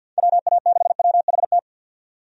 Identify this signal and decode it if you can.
Morse code, heard as a keyed tone, about 51 wpm, 700 Hz. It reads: WA6WHT
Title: USA Call Signs - 6 Characters - 50wpm